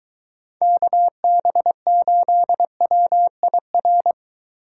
Morse code, heard as a keyed tone, about 23 wpm, 700 Hz. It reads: KB8WIR